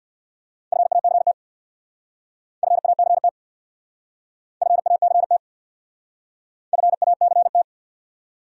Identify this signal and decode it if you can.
Morse code, heard as a keyed tone, about 65 wpm, 700 Hz. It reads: FACT FACT FACT FACT